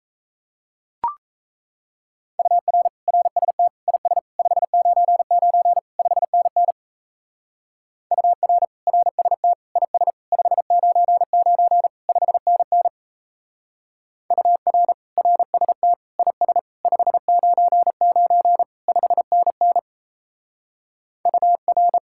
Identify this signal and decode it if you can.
Morse code, heard as a keyed tone, about 33 wpm, 700 Hz. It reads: UR RST IS 599 5NN UR RST IS 599 5NN UR RST IS 599 5NN UR
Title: Speed-Racing Common QSO Phrases - 28wpm